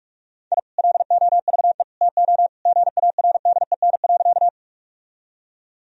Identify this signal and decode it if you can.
Morse code, heard as a keyed tone, about 46 wpm, 700 Hz. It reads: I LOVE TO GARDEN.
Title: Sentences from Top 1000 Words - Review Part 4 - 45wpm